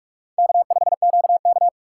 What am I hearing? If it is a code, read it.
Morse code, heard as a keyed tone, about 46 wpm, 700 Hz. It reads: K5QK
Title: USA Call Signs - 4 Characters - 3x Character Spacing - 45wpm